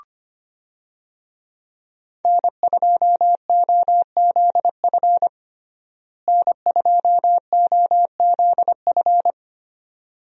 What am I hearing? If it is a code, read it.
Morse code, heard as a keyed tone, about 25 wpm, 700 Hz. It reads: N2OZF N2OZF
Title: USA Call Signs - 5 Characters - Repeated 2x - 25wpm